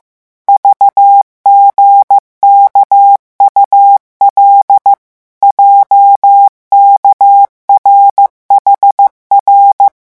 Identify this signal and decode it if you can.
Morse code, heard as a keyed tone, about 15 wpm, 800 Hz. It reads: VGKUL JKRHR
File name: morse mp3 / revision11 lettres vitesse 14